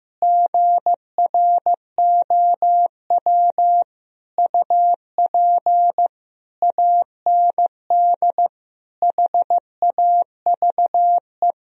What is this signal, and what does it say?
Morse code, heard as a keyed tone, about 16 wpm, 700 Hz. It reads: GROW UP AND HAVE
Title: Sentences from Top 1000 Words - Review Part 1 - 15wpm